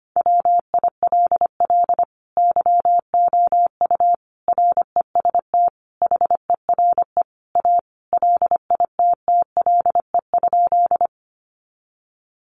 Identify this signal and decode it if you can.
Morse code, heard as a keyed tone, about 25 wpm, 700 Hz. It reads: WILL YOU REST HERE A LITTLE?
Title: Sentences from Top 500 Words 25wpm